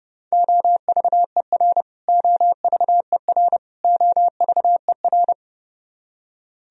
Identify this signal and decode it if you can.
Morse code, heard as a keyed tone, about 30 wpm, 700 Hz. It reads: OVER OVER OVER
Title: Top 100 Words Repeated 30wpm